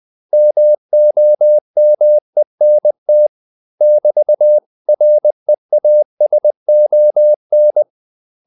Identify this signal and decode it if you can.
Morse code, heard as a keyed tone, about 20 wpm, 600 Hz. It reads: MOMENT <BT>REASON
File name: CW / Practice / 500MostCommonEnglishWords20